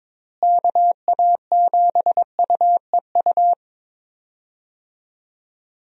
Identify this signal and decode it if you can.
Morse code, heard as a keyed tone, about 21 wpm, 700 Hz. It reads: KA7UEU
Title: USA Call Signs - 6 Characters - Repeated 2x - 22wpm